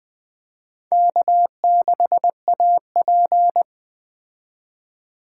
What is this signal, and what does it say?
Morse code, heard as a keyed tone, about 19 wpm, 700 Hz. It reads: K6AP